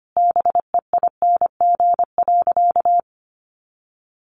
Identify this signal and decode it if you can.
Morse code, heard as a keyed tone, about 25 wpm, 700 Hz. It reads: BEING.
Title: Sentences from Top 1000 Words 25wpm